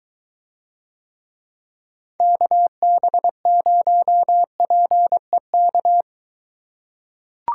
Morse code, heard as a keyed tone, about 22 wpm, 700 Hz. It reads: KB0PEK